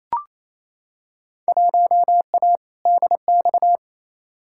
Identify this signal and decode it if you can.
Morse code, heard as a keyed tone, about 28 wpm, 700 Hz. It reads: 1A DX
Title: Field Day Exchanges Encore - 28wpm